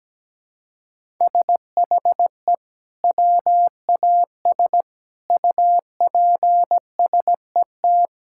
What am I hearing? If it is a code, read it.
Morse code, heard as a keyed tone, about 17 wpm, 700 Hz. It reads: SHE WAS UPSET
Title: Sentences from Top 1000 Words - Review Part 5 - 17wpm